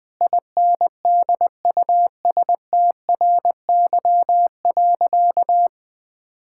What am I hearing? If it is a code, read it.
Morse code, heard as a keyed tone, about 20 wpm, 700 Hz. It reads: INDUSTRY.
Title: Sentences with 8-Word Phrases - 20wpm